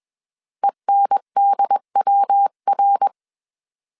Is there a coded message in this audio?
Yes: Morse code from 0.6 to 3.1 s, heard as a keyed tone, about 21 wpm, 800 Hz. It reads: ENDWR